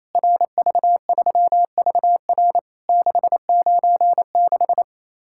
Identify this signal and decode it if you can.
Morse code, heard as a keyed tone, about 27 wpm, 700 Hz. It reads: RV3VR 696
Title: Worked All Europe DX Contest - QTC Exchange - 28wpm